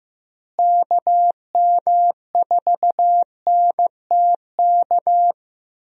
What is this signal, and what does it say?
Morse code, heard as a keyed tone, about 16 wpm, 700 Hz. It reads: KM4NTK
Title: USA Call Signs - 6 Characters - Repeated 2x - 15wpm